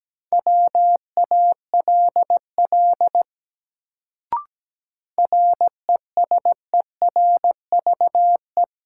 Morse code, heard as a keyed tone, about 17 wpm, 700 Hz. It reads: WALL RESERVE
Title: Sentences from Top 1000 Words - Review Part 6 - New Word Drill - 17wpm